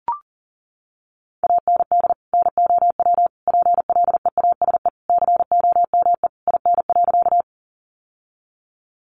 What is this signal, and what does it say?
Morse code, heard as a keyed tone, about 40 wpm, 700 Hz. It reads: AND NOW PLEASE COME IN.
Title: Sentences from Top 1500 Words 40wpm